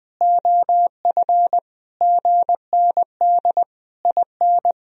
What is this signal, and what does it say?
Morse code, heard as a keyed tone, about 20 wpm, 700 Hz. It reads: OF GND IN
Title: Sentences from Top 500 Words and Abbreviations 20wpm